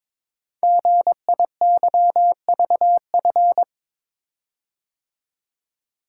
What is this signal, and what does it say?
Morse code, heard as a keyed tone, about 22 wpm, 700 Hz. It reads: GIYVF
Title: Code Groups - 5 Letters - 22wpm